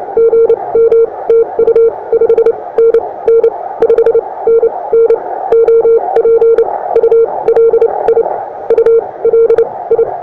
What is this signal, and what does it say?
Morse code, heard as a keyed tone, about 29 wpm, spaced as for 21 wpm, 460 Hz. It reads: GM TU 5NN 5NN OP ULI ULI
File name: attachments / europe 20m